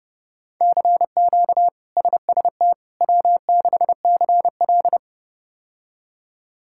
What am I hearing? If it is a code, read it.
Morse code, heard as a keyed tone, about 32 wpm, 700 Hz. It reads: CQ SST W6CL
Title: K1USN SST QSO Practice - 30wpm